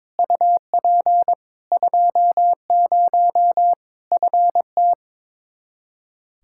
Morse code, heard as a keyed tone, about 21 wpm, 700 Hz. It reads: UP 20 FT
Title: Practice QSO - Exchange 2 - Info - 22wpm